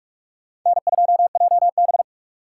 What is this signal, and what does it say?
Morse code, heard as a keyed tone, about 46 wpm, 700 Hz. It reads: N2JB
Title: USA Call Signs - 4 Characters - 2x Character Spacing - 45wpm